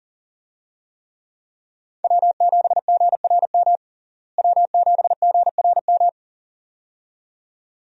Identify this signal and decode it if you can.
Morse code, heard as a keyed tone, about 40 wpm, 700 Hz. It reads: W7GRM W7GRM
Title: USA Call Signs - 5 Characters - Repeated 2x - 40wpm